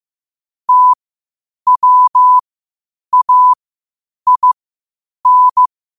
Morse code, heard as a keyed tone, about 15 wpm, spaced as for 8 wpm, 1000 Hz. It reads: TWAIN
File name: qotdfiles / 8wpm